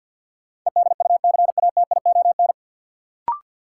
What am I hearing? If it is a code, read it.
Morse code, heard as a keyed tone, about 53 wpm, 700 Hz. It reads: EDUCATION
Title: Sentences from Top 1100 Words - New Word Drill Alt - 50wpm